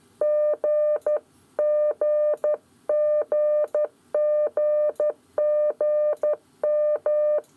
Morse code, heard as a keyed tone, about 12 wpm, 600 Hz. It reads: G GGGGM
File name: 7 Esercizio qgzl